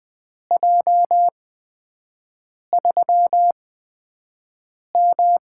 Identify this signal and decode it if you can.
Morse code, heard as a keyed tone, about 20 wpm, spaced as for 5 wpm, 700 Hz. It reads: J3M
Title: USA Call Signs - 8x Character Spacing - 20wpm